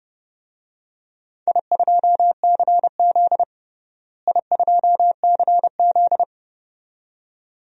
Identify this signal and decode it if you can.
Morse code, heard as a keyed tone, about 32 wpm, 700 Hz. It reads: I2CZ I2CZ
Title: DXCC-100 Call Signs - 4 Characters - Repeated 2x - 30wpm